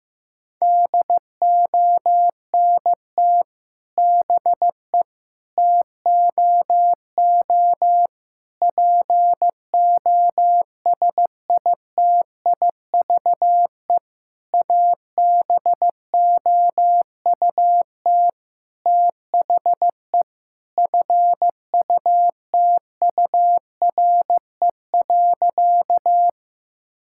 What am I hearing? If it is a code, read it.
Morse code, heard as a keyed tone, about 15 wpm, 700 Hz. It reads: DONT BE TOO POSITIVE ABOUT THE FUTURE.
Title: Sentences from Top 1000 Words - Review Part 5 - 15wpm